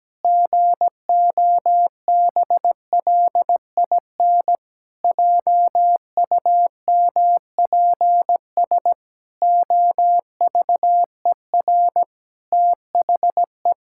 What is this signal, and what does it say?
Morse code, heard as a keyed tone, about 17 wpm, 700 Hz. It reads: GOBLIN JUMPS OVER THE
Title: Pangram Challenge v2 - 17wpm